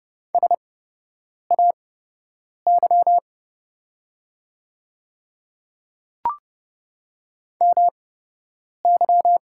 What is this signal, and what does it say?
Morse code, heard as a keyed tone, about 30 wpm, spaced as for 8 wpm, 700 Hz. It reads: SAY MY
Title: 8x Character Spacing - Top 200 Words - 30wpm